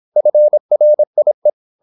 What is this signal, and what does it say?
Morse code, heard as a keyed tone, about 26 wpm, 600 Hz. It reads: FRIE
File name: Korte ord DK 26wpm